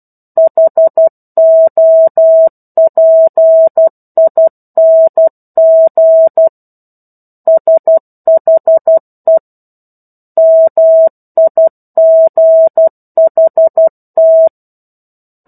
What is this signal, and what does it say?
Morse code, heard as a keyed tone, about 12 wpm, 650 Hz. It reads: HOPING SHE MIGHT
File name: 01-Chapter-01 12 WPM